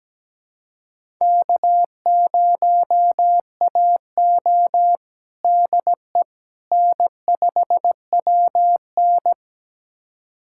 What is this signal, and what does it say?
Morse code, heard as a keyed tone, about 17 wpm, 700 Hz. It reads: K0AO DE N5WN
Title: USA Callsign DE Callsign - 4 Characters - 17wpm